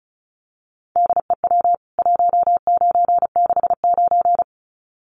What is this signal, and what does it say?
Morse code, heard as a keyed tone, about 34 wpm, 700 Hz. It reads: DEW 1969
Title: CWT Exchange 35wpm